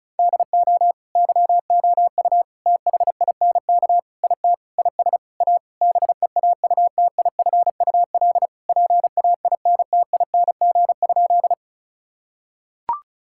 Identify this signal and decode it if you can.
Morse code, heard as a keyed tone, about 36 wpm, 700 Hz. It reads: DO YOU THINK IT IS A BEAUTIFUL PAINTING?
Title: Sentences from Top 1000 Words - Review Part 6 - 35wpm